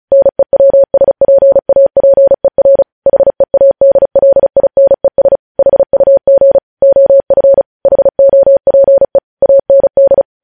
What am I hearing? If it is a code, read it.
Morse code, heard as a keyed tone, about 36 wpm, 550 Hz. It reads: NEWSPAPER HEADLINES HUG OF HOPE AND
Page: News Headlines in Morse Code at 35 WPM